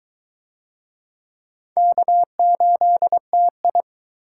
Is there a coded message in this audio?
Yes: Morse code from 1.8 to 3.8 s, heard as a keyed tone, about 22 wpm, 700 Hz. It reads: K8TI